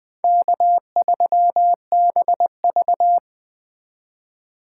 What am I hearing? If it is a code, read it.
Morse code, heard as a keyed tone, about 20 wpm, 700 Hz. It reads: K3BV
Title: Speed Racing - USA 1x2 Call Signs 17wpm